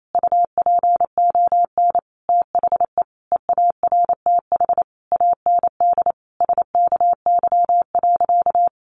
Morse code, heard as a keyed tone, about 28 wpm, 700 Hz. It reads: UPON THE EARTH AND SKY.
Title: Sentences from Top 1200 Words 28wpm